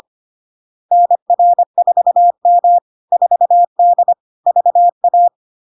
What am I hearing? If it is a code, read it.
Morse code, heard as a keyed tone, about 25 wpm, 700 Hz. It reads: NR4M 4D VA
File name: Log-in-CW-25wpm-700-Hz